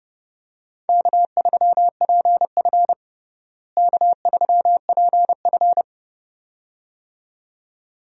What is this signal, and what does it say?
Morse code, heard as a keyed tone, about 32 wpm, 700 Hz. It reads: K3PF K3PF
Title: USA Call Signs - 4 Characters - Repeated 2x - 30wpm